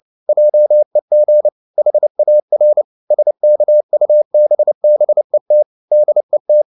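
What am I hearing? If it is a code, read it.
Morse code, heard as a keyed tone, about 29 wpm, 600 Hz. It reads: JEG HAR SKUBBET DET
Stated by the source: Fast (QRQ) Morse at 29 wpm, from CW med Gnister.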